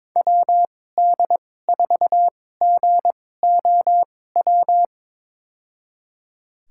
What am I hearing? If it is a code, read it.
Morse code, heard as a keyed tone, about 22 wpm, spaced as for 16 wpm, 700 Hz. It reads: WD4GOW
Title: USA Call Signs - 6 Characters - 2x Character Spacing - 22wpm